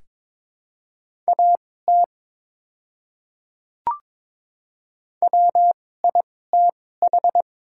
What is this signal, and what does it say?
Morse code, heard as a keyed tone, about 22 wpm, spaced as for 16 wpm, 700 Hz. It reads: AT WITH
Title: 2x Character Spacing - Top 200 Words - 22wpm